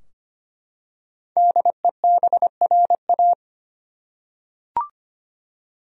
Morse code, heard as a keyed tone, about 25 wpm, 700 Hz. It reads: DEBRA
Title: USA Common Names - 25wpm